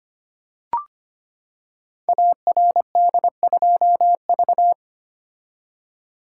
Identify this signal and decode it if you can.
Morse code, heard as a keyed tone, about 25 wpm, 700 Hz. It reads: ARD2V
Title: Code Groups - 5 Letter:Numbers - 25wpm